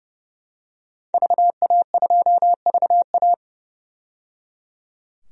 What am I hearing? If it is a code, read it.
Morse code, heard as a keyed tone, about 30 wpm, 700 Hz. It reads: VA2VA
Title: DXCC-100 Call Signs - 5 Characters - 30wpm